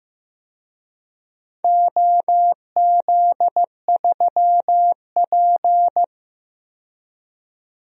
Morse code, heard as a keyed tone, about 16 wpm, 700 Hz. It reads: OZ3P
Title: DXCC-100 Call Signs - 4 Characters - 15wpm